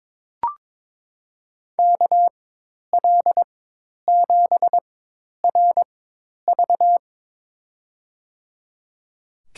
Morse code, heard as a keyed tone, about 23 wpm, spaced as for 10 wpm, 700 Hz. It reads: KL7RV
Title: USA Call Signs - 4x Character Spacing - 22wpm